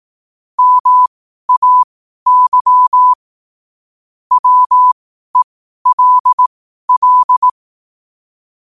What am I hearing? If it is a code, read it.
Morse code, heard as a keyed tone, about 18 wpm, spaced as for 12 wpm, 1000 Hz. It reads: MAY WELL
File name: qotdfiles / 12wpm